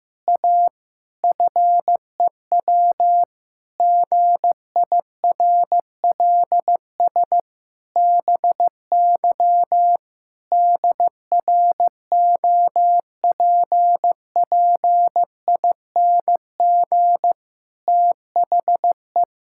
Morse code, heard as a keyed tone, about 15 wpm, 700 Hz. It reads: A FEW GIRLS BY DROPPING THE
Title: Pangram Challenge v2 - 15wpm